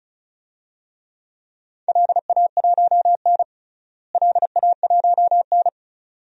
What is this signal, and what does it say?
Morse code, heard as a keyed tone, about 34 wpm, 700 Hz. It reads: LA1N LA1N
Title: DXCC-100 Call Signs - 4 Characters - Repeated 2x - 35wpm